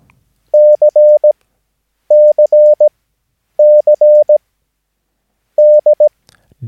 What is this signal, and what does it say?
Morse code, heard as a keyed tone, about 17 wpm, spaced as for 9 wpm, 600 Hz. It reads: CCC D
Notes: Övningsmaterial med alfabetet A till Ö samt siffror 0 till 9 enligt International/Continental Morse Code.